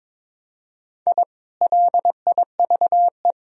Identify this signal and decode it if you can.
Morse code, heard as a keyed tone, about 22 wpm, 700 Hz. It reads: I LIVE
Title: Sentences from Top 1000 Words - Review Part 1 - 22wpm